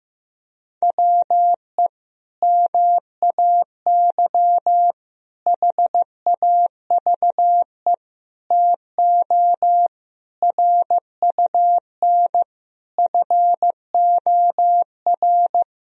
Morse code, heard as a keyed tone, about 15 wpm, 700 Hz. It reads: WE MAY HAVE TO RUN FOR
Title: Sentences from Top 1000 Words - Review Part 1 - 15wpm